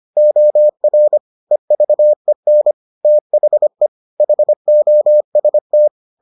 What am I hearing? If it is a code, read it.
Morse code, heard as a keyed tone, about 25 wpm, 600 Hz. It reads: OR EVEN THE HOST